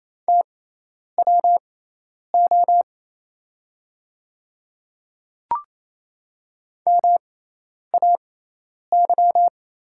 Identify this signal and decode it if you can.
Morse code, heard as a keyed tone, about 28 wpm, spaced as for 10 wpm, 700 Hz. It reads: TWO MAY
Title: 6x Character Spacing - Top 300 Words - 28wpm